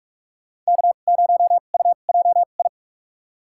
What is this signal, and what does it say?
Morse code, heard as a keyed tone, about 42 wpm, spaced as for 31 wpm, 700 Hz. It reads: K0UJI
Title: USA Call Signs - 5 Characters - 2x Character Spacing - 45wpm